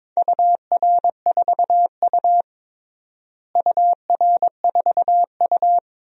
Morse code, heard as a keyed tone, about 22 wpm, 700 Hz. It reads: UR4U UR4U
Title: DXCC-100 Call Signs - 4 Characters - Repeated 2x - 22wpm